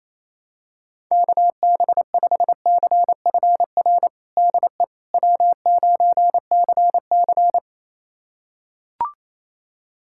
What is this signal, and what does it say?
Morse code, heard as a keyed tone, about 27 wpm, 700 Hz. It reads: KB5CFR DE W9CC
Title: USA Callsign DE Callsign 28wpm